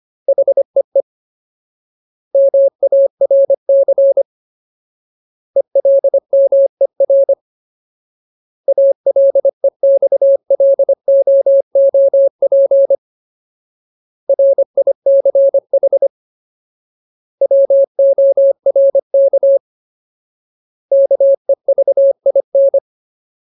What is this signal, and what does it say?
Morse code, heard as a keyed tone, about 25 wpm, 550 Hz. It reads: HEE MARC ELMER ALEXLOOP RICH WORK KEVIN
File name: Ultimate Word List / 25WPM Ultimate